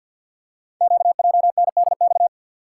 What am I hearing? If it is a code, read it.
Morse code, heard as a keyed tone, about 50 wpm, 700 Hz. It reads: QJNDX
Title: Code Groups - 5 Letters - 50wpm